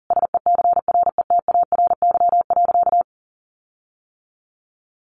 Morse code, heard as a keyed tone, about 40 wpm, 700 Hz. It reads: SECRETARY.
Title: Sentences from Top 1300 Words 40wpm